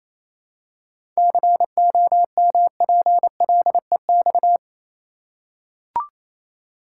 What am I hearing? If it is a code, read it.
Morse code, heard as a keyed tone, about 28 wpm, 700 Hz. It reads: COMPLEX
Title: Seven Letter Words - 28wpm